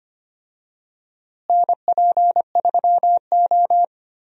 Morse code, heard as a keyed tone, about 25 wpm, 700 Hz. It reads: NP3O
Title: USA Call Signs - 4 Characters - Repeated 2x - 25wpm